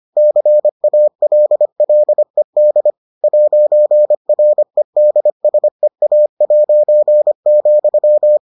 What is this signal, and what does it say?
Morse code, heard as a keyed tone, about 25 wpm, 600 Hz. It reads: CALLED 'REDSEA',